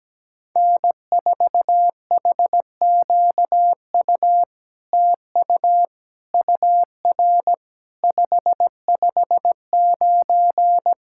Morse code, heard as a keyed tone, about 17 wpm, 700 Hz. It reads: N4HQU TU UR 559
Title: SOTA QSO Practice 17wpm